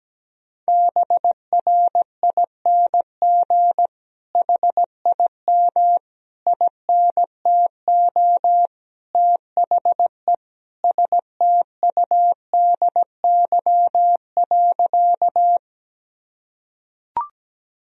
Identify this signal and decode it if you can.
Morse code, heard as a keyed tone, about 17 wpm, 700 Hz. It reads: BRING HIM INTO THE STUDY.
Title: Sentences from Top 1000 Words - Review Part 1 - 17wpm